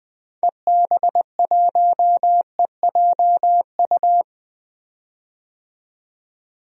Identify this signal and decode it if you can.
Morse code, heard as a keyed tone, about 21 wpm, 700 Hz. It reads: EB1EJU
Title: DXCC-100 Call Signs - 6 Characters - 20wpm